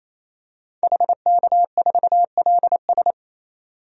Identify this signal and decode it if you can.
Morse code, heard as a keyed tone, about 28 wpm, 700 Hz. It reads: HK4LS
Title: DXCC-100 Call Signs - 5 Characters - 28wpm